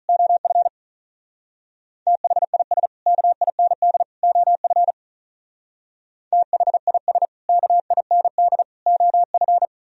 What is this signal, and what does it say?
Morse code, heard as a keyed tone, about 44 wpm, 700 Hz. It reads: OF THIS KIND OF THIS KIND OF